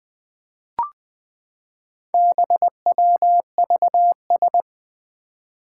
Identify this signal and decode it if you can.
Morse code, heard as a keyed tone, about 20 wpm, 700 Hz. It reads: BWVS
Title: Four Challenging Letter-Number Combinations 20wpm